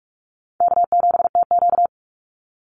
Morse code, heard as a keyed tone, about 44 wpm, 700 Hz. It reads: K7TQ
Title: NA SOTA Chaser Callsigns 45wpm